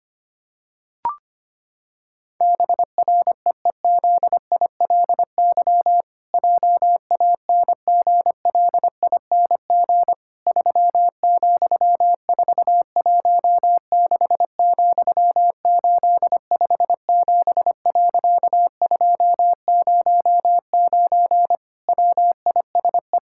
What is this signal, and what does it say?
Morse code, heard as a keyed tone, about 25 wpm, 700 Hz. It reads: BREEZILY JANGLING 3,416,857.209 WISE
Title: Pangram Challenge v2 - 25wpm